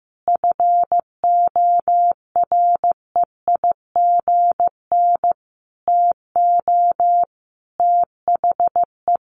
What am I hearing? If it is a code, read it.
Morse code, heard as a keyed tone, about 15 wpm, 700 Hz. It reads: FOREIGN TO THE
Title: Sentences from Top 1400 Words 15wpm